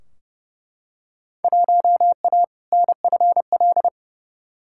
Morse code, heard as a keyed tone, about 30 wpm, 700 Hz. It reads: 1A NFL
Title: Field Day Exchanges - 30wpm